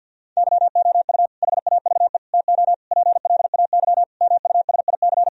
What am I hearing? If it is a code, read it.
Morse code, heard as a keyed tone, about 52 wpm, 700 Hz. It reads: YOU HAVE TO PLAY MUSIC
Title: Sentences from Top 1000 Words - Review Part 1 - 50wpm